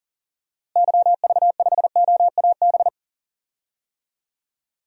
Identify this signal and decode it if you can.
Morse code, heard as a keyed tone, about 40 wpm, 700 Hz. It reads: YV5OAB
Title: DXCC-100 Call Signs - 6 Characters - Repeated 3x - 40wpm